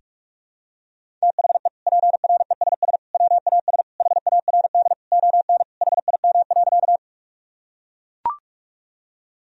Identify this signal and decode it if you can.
Morse code, heard as a keyed tone, about 41 wpm, 700 Hz. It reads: THE PRESS WAS HARD ON HIM.
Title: Sentences from Top 1000 Words - Review Part 1 - 45wpm